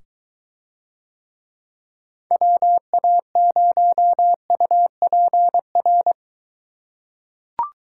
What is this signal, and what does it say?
Morse code, heard as a keyed tone, about 22 wpm, 700 Hz. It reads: WA0UPR